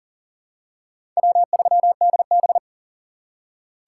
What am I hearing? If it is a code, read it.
Morse code, heard as a keyed tone, about 40 wpm, 700 Hz. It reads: W3DB
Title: USA Call Signs - 4x Character Spacing - 40wpm